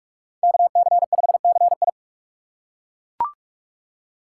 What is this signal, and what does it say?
Morse code, heard as a keyed tone, about 42 wpm, 700 Hz. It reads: KC5CI